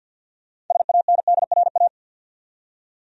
Morse code, heard as a keyed tone, about 50 wpm, 700 Hz. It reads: SANDRA
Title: USA Common Names - 50wpm